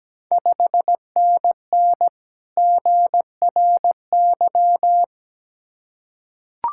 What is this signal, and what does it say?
Morse code, heard as a keyed tone, about 17 wpm, 700 Hz. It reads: 5NN GRY
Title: Ontario QSO Party Exchange 17wpm